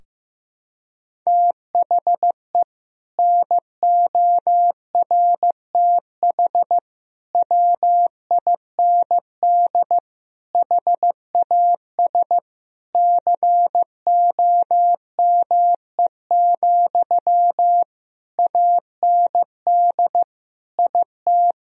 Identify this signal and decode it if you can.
Morse code, heard as a keyed tone, about 15 wpm, 700 Hz. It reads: THE NORTH WIND HAS COME, AND IT
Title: Sentences from Top 1000 Words - Review Part 3 - 15wpm